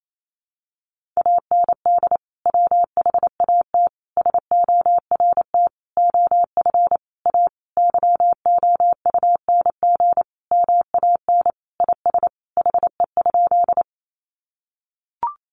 Morse code, heard as a keyed tone, about 28 wpm, 700 Hz. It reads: AND WHAT SORT OF A YOUNG MAN IS HE?
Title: Sentences from Top 1300 Words 28wpm